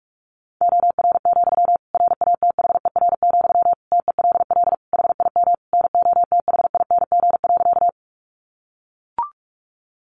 Morse code, heard as a keyed tone, about 44 wpm, 700 Hz. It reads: OR, RATHER, TELL HIM NOTHING.
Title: Sentences from Top 700 Words 45wpm